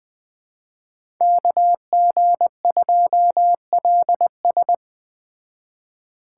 Morse code, heard as a keyed tone, about 20 wpm, 700 Hz. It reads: KG2LS